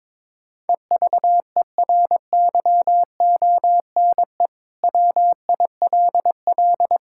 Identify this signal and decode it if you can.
Morse code, heard as a keyed tone, about 22 wpm, 700 Hz. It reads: EVERYONE WILL
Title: Sentences from Top 1000 Words - Review Part 5 - 22wpm